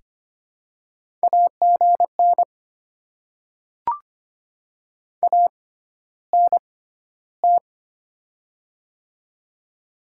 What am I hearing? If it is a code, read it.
Morse code, heard as a keyed tone, about 25 wpm, spaced as for 13 wpm, 700 Hz. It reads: AGN ANT
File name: Top 20 QSO Elements - 6x Character Spacing - 25wpm